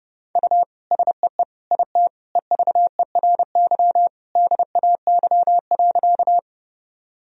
Morse code, heard as a keyed tone, about 30 wpm, 700 Hz. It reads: U SEE IT EVERY DAY.
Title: Sentences from Top 200 Words and Abbreviations 30wpm